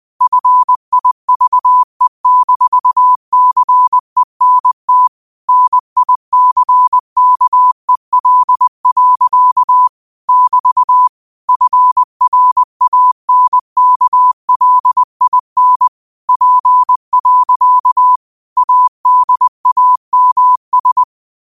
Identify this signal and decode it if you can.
Morse code, heard as a keyed tone, about 20 wpm, 1000 Hz. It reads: FIVE-CENT NICKEL. <BT> FRANKLIN P. ADAMS
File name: qotdfiles / 20wpm